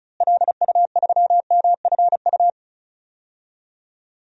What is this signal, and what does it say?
Morse code, heard as a keyed tone, about 34 wpm, 700 Hz. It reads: LU3MFU
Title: DXCC-100 Call Signs - 6 Characters - Repeated 2x - 35wpm